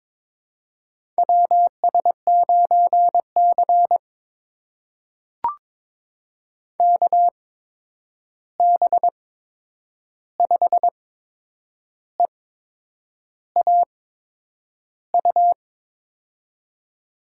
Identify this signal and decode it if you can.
Morse code, heard as a keyed tone, about 22 wpm, spaced as for 8 wpm, 700 Hz. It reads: WS9C KB5EAU
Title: USA Call Signs - 8x Character Spacing - 22wpm